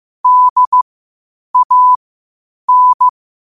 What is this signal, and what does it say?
Morse code, heard as a keyed tone, about 15 wpm, spaced as for 8 wpm, 1000 Hz. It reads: DAN
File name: qotdfiles / 8wpm